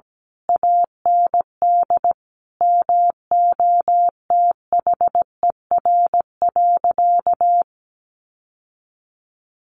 Morse code, heard as a keyed tone, about 17 wpm, 700 Hz. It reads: AND MOTHER.
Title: Sentences from Top 1100 Words 17wpm